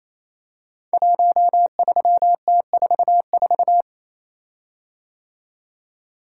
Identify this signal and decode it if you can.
Morse code, heard as a keyed tone, about 29 wpm, 700 Hz. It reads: 13T44
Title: Code Groups - 5 Letter:Numbers - 28wpm